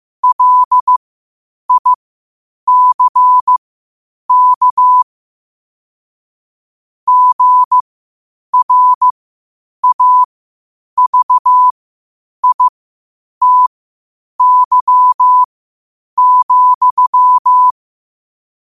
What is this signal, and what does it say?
Morse code, heard as a keyed tone, about 15 wpm, spaced as for 8 wpm, 1000 Hz. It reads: LICK GRAVITY,
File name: 8wpm